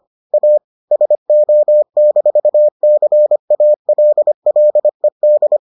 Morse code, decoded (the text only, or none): A SO-CALLED